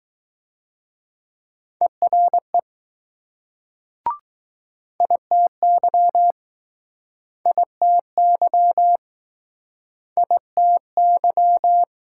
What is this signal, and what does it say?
Morse code, heard as a keyed tone, about 19 wpm, 700 Hz. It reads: ERE ITY ITY ITY